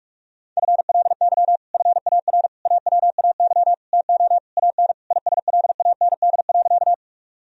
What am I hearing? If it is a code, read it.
Morse code, heard as a keyed tone, about 46 wpm, 700 Hz. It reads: FLY FAR AWAY TO AN ISLAND.
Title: Sentences from Top 1000 Words - Review Part 2 - 45wpm